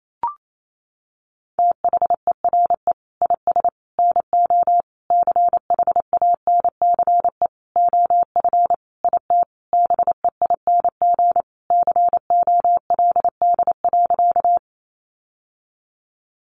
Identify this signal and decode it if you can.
Morse code, heard as a keyed tone, about 28 wpm, 700 Hz. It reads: THERE IS NO CHANCE OF IT BEING COLD.
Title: Sentences from Top 800 Words 28wpm